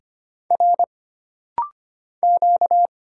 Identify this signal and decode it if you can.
Morse code, heard as a keyed tone, about 25 wpm, 700 Hz. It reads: R Q
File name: Single Character (Letter:Number:Punctuation:Prosign) - Rapid-Fire - 25wpm